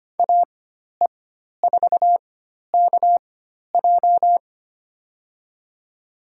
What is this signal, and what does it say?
Morse code, heard as a keyed tone, about 25 wpm, spaced as for 12 wpm, 700 Hz. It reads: AE4KJ
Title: USA Call Signs - 5 Characters - 4x Character Spacing - 25wpm